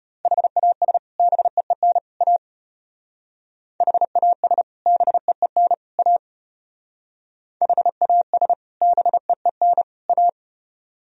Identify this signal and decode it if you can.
Morse code, heard as a keyed tone, about 33 wpm, 700 Hz. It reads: HAS BEEN A HAS BEEN A HAS BEEN A